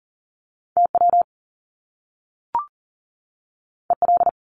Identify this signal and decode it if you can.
Morse code, heard as a keyed tone, about 39 wpm, 700 Hz. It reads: TW EL
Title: Top Two Letter Combinations in Words 40wpm